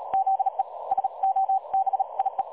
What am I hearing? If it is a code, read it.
Morse code, heard as a keyed tone, about 37 wpm, 770 Hz. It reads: ZE SOBIE
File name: dlazaawansowanych / nagraniazpasma / websdr recording 3565.0kHz